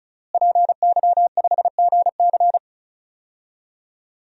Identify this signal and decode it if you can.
Morse code, heard as a keyed tone, about 34 wpm, 700 Hz. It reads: PY5GC
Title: DXCC-100 Call Signs - 5 Characters - Repeated 2x - 35wpm